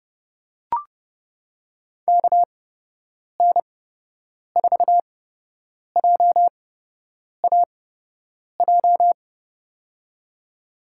Morse code, heard as a keyed tone, about 30 wpm, spaced as for 8 wpm, 700 Hz. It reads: KN4JAJ
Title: USA Call Signs - 6 Characters - 8x Character Spacing - 30wpm